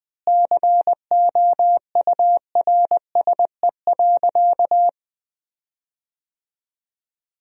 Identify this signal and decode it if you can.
Morse code, heard as a keyed tone, about 20 wpm, 700 Hz. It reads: COURSE.
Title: 4x Word-Spacing - Sentences from Top 500 Words - 20wpm